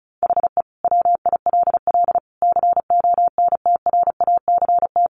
Morse code, heard as a keyed tone, about 34 wpm, 700 Hz. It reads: HE WILL CONTRACT
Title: Sentences from Top 1400 Words 35wpm